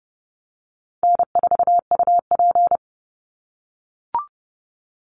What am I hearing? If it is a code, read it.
Morse code, heard as a keyed tone, about 30 wpm, 700 Hz. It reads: N4UP
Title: Easy Call Signs - 30wpm